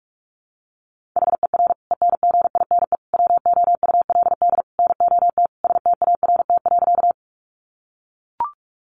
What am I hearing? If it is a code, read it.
Morse code, heard as a keyed tone, about 44 wpm, 700 Hz. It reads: HER ENGINE WOULD NOT START.
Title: Sentences from Top 1500 Words 45wpm